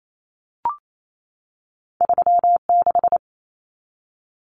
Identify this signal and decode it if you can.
Morse code, heard as a keyed tone, about 28 wpm, 700 Hz. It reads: 3 6
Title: Common Cut Numbers - 2-Digit 28wpm